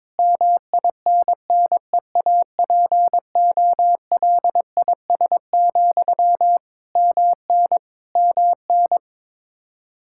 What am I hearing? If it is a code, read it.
Morse code, heard as a keyed tone, about 22 wpm, 700 Hz. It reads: MINNEAPOLIS, MN MN
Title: Practice QSO - Exchange 1 - Beginning - 22wpm